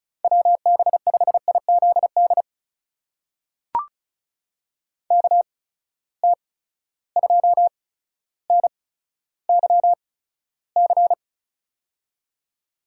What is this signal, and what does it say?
Morse code, heard as a keyed tone, about 34 wpm, spaced as for 15 wpm, 700 Hz. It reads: WB5IZD KT2NYC
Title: USA Call Signs - 6 Characters - 8x Character Spacing - 35wpm